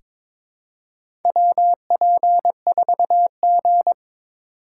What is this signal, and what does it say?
Morse code, heard as a keyed tone, about 21 wpm, 700 Hz. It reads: WP4G
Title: USA Call Signs - 4x Character Spacing - 22wpm